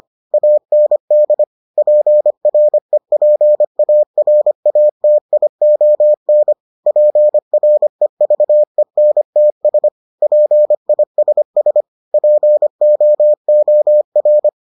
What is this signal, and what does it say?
Morse code, heard as a keyed tone, about 25 wpm, 600 Hz. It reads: AND PREPARATION PREVENTS PISS POOR